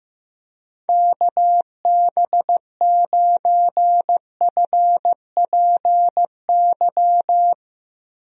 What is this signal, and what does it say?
Morse code, heard as a keyed tone, about 15 wpm, 700 Hz. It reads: KB9FPY
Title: USA Call Signs - 6 Characters - Repeated 3x - 15wpm